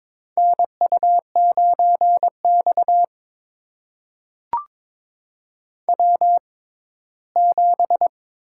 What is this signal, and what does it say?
Morse code, heard as a keyed tone, about 22 wpm, spaced as for 15 wpm, 700 Hz. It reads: NU9X W7
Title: USA Call Signs - 4 Characters - 6x Character Spacing - 22wpm